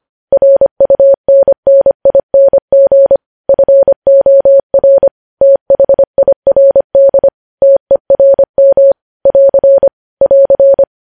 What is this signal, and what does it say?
Morse code, heard as a keyed tone, about 25 wpm, 550 Hz. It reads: RUNNING FOR THIRD TERM <AR> <AR>
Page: News Headlines in Morse Code at 25 WPM